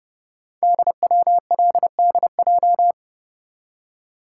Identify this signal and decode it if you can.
Morse code, heard as a keyed tone, about 30 wpm, 700 Hz. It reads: DWLDJ
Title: Code Groups - 5 Letters - 30wpm